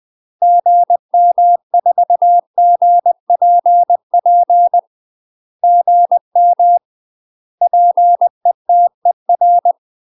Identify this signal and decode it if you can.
Morse code, heard as a keyed tone, about 20 wpm, 700 Hz. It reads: GM4GPP GM PETER
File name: UK MASTER QSOs TRAINING / 20WPM / UK MASTER QSOs 0048